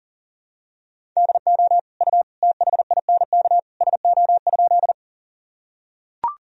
Morse code, heard as a keyed tone, about 37 wpm, 700 Hz. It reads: DO U THINK SO?
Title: Sentences from Top 200 Words and Abbreviations 40wpm